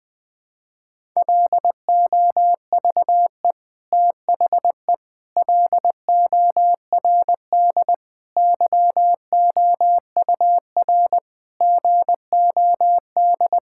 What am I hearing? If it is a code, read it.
Morse code, heard as a keyed tone, about 20 wpm, 700 Hz. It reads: LOVE THE LORD YOUR GOD
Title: Sets of 9 Words - 20wpm